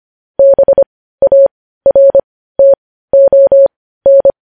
Morse code, heard as a keyed tone, about 25 wpm, spaced as for 15 wpm, 550 Hz. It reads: BARTON
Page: news-headlines-in-morse-code-at-15-wpm